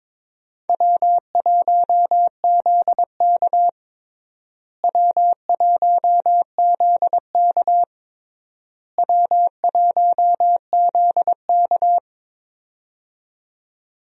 Morse code, heard as a keyed tone, about 22 wpm, 700 Hz. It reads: W1ZK W1ZK W1ZK
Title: USA Call Signs - 4 Characters - Repeated 3x - 22wpm